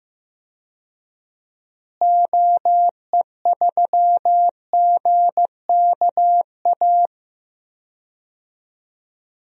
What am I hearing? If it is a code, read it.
Morse code, heard as a keyed tone, about 15 wpm, 700 Hz. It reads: OE3GKA
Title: DXCC-100 Call Signs - 6 Characters - Repeated 2x - 15wpm